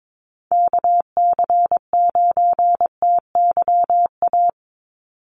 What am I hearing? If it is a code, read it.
Morse code, heard as a keyed tone, about 22 wpm, 700 Hz. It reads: KC9TYA
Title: NA SOTA Chaser Callsigns 22wpm